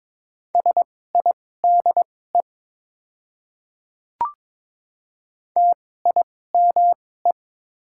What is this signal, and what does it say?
Morse code, heard as a keyed tone, about 21 wpm, spaced as for 16 wpm, 700 Hz. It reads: SIDE TIME
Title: 2x Character Spacing - Top 200 Words - 22wpm